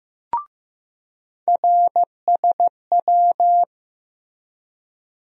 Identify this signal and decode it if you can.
Morse code, heard as a keyed tone, about 16 wpm, 700 Hz. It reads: RSW
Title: USA Call Sign Postfixes - 3 Letters 15wpm